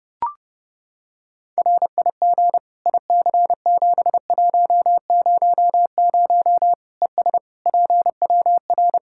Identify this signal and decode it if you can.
Morse code, heard as a keyed tone, about 30 wpm, 700 Hz. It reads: RIG IC7100 ES PWR
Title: K1USN SST QSO Practice 30wpm